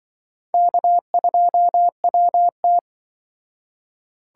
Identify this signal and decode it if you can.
Morse code, heard as a keyed tone, about 25 wpm, 700 Hz. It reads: K2WT
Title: Speed Racing - USA 1x2 Call Signs 20wpm